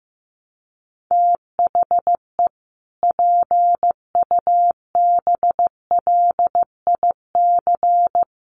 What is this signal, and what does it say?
Morse code, heard as a keyed tone, about 14 wpm, 700 Hz. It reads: THE PUBLIC
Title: Sentences from Top 1300 Words 15wpm